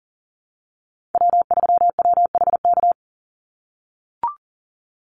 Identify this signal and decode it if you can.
Morse code, heard as a keyed tone, about 40 wpm, 700 Hz. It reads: W3WHK
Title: Call Signs - 40wpm